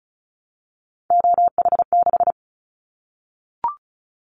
Morse code, heard as a keyed tone, about 34 wpm, 700 Hz. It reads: OH6
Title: Callsign Prefixes for Top 25 DXCC - Repeated 3x - 35wpm